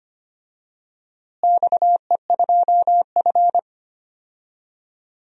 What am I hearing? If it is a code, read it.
Morse code, heard as a keyed tone, about 25 wpm, 700 Hz. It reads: XE2F
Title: DXCC-100 Call Signs - 4 Characters - Repeated 3x - 25wpm